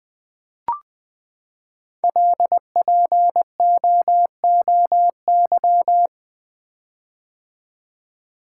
Morse code, heard as a keyed tone, about 20 wpm, 700 Hz. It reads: LPOOY
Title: Code Groups - 5 Letters - 20wpm